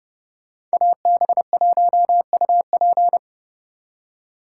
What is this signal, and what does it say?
Morse code, heard as a keyed tone, about 35 wpm, 700 Hz. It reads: AB1UP
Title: Speed Racing - USA 2x2 Call Signs 30wpm